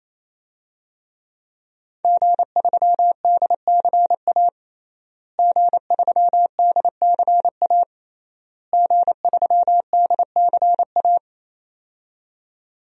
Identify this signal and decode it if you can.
Morse code, heard as a keyed tone, about 28 wpm, 700 Hz. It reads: G3DCA G3DCA G3DCA
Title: DXCC-100 Call Signs - 5 Characters - Repeated 3x - 28wpm